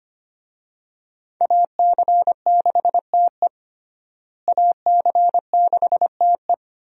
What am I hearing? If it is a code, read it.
Morse code, heard as a keyed tone, about 25 wpm, 700 Hz. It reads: AC6TE AC6TE
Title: USA Call Signs - 5 Characters - Repeated 2x - 25wpm